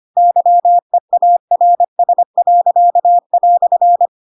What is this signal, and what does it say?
Morse code, heard as a keyed tone, about 25 wpm, 700 Hz. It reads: YEARS."
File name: Pinocchio / Pinocchio ch4 25 25 ik1hge